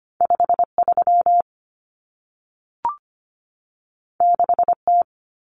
Morse code, heard as a keyed tone, about 25 wpm, 700 Hz. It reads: 53 6T
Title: Common Cut Numbers - 2-Digit 25wpm